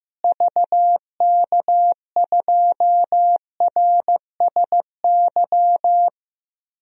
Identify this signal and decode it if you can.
Morse code, heard as a keyed tone, about 15 wpm, 700 Hz. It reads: VK2RSY
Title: DXCC-100 Call Signs - 6 Characters - Repeated 2x - 15wpm